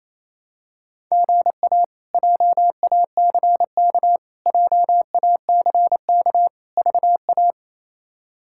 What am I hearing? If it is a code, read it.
Morse code, heard as a keyed tone, about 28 wpm, 700 Hz. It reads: GA JACK JACK VA
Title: K1USN SST QSO Practice - 28wpm